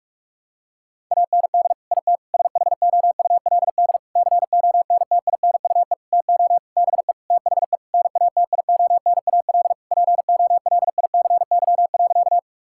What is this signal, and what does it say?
Morse code, heard as a keyed tone, about 46 wpm, 700 Hz. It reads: AND IT SHOULD CONTINUE TO BE THE NATIONAL POLICY.
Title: Sentences from Top 1000 Words - Review Part 6 - 45wpm